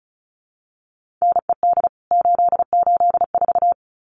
Morse code, heard as a keyed tone, about 34 wpm, 700 Hz. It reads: NED 884
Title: CWT Exchange 35wpm